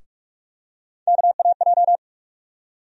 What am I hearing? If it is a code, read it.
Morse code, heard as a keyed tone, about 44 wpm, 700 Hz. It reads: KAJ
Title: USA Call Sign Postfixes - 3 Letters 45wpm